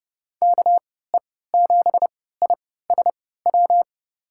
Morse code, heard as a keyed tone, about 30 wpm, spaced as for 17 wpm, 700 Hz. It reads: KE7ISW
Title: USA Call Signs - 6 Characters - 3x Character Spacing - 30wpm